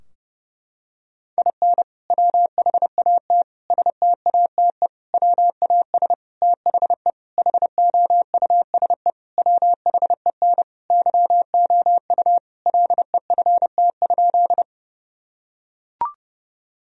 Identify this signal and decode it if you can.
Morse code, heard as a keyed tone, about 32 wpm, 700 Hz. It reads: IN WHAT STATE WAS THE HOUSE WHEN YOU LEFT?
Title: Sentences from Top 1000 Words - Review Part 2 - 30wpm